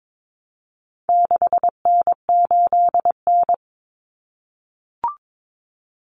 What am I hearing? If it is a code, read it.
Morse code, heard as a keyed tone, about 22 wpm, 700 Hz. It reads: 6N8N
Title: Common Cut Numbers - 4-Digit 22wpm